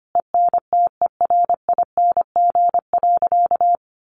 Morse code, heard as a keyed tone, about 25 wpm, 700 Hz. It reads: ENTERING.
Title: A Sound Of Thunder 25wpm